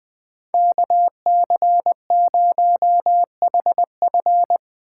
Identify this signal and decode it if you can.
Morse code, heard as a keyed tone, about 20 wpm, 700 Hz. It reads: KC0HF
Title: Speed Racing - USA 2x2 Call Signs 17wpm